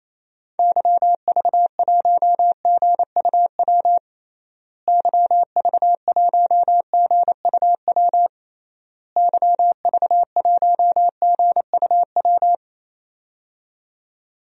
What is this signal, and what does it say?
Morse code, heard as a keyed tone, about 28 wpm, 700 Hz. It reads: YV1GUW YV1GUW YV1GUW
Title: DXCC-100 Call Signs - 6 Characters - Repeated 3x - 28wpm